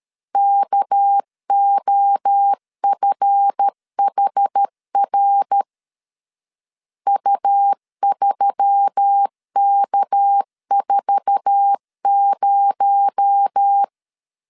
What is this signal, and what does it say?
Morse code, heard as a keyed tone, about 13 wpm, 800 Hz. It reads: KOFHR U3K40
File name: LEZIONI / 8 YJ45 / 8 YJ4 12wpm 2151